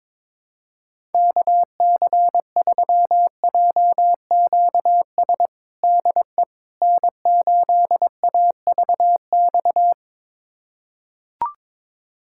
Morse code, heard as a keyed tone, about 22 wpm, 700 Hz. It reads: KC3JQS DE N8AVX
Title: USA Callsign DE Callsign 22wpm